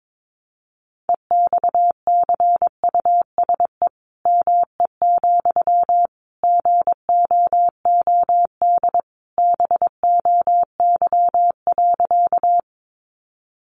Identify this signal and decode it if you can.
Morse code, heard as a keyed tone, about 22 wpm, 700 Hz. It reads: EXCUSE ME, GOOD BOY.
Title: Sentences from Top 700 Words 22wpm